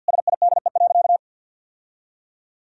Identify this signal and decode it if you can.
Morse code, heard as a keyed tone, about 50 wpm, 700 Hz. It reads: SIDE.
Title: Sentences from Top 1000 Words - Review Part 3 - 50wpm